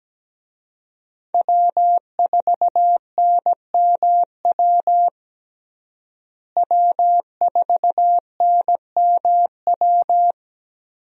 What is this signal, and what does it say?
Morse code, heard as a keyed tone, about 17 wpm, 700 Hz. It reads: W4NMW W4NMW
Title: USA Call Signs - 5 Characters - Repeated 3x - 17wpm